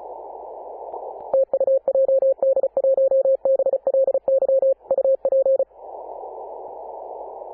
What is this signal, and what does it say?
Morse code, heard as a keyed tone, about 35 wpm, 540 Hz. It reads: TU JD1BLY UP
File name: Apr/29 2042z JD1BLY 14.011MHz CW